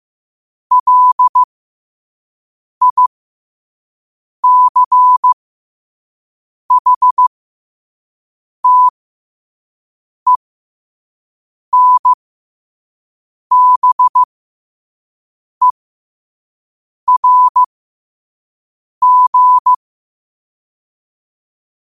Morse code, heard as a keyed tone, about 15 wpm, spaced as for 5 wpm, 1000 Hz. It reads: LICHTENBERG
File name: qotdfiles / 5wpm